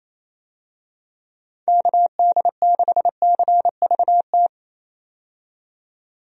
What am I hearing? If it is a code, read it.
Morse code, heard as a keyed tone, about 27 wpm, 700 Hz. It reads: KD6CVT
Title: USA Call Signs - 6 Characters - 28wpm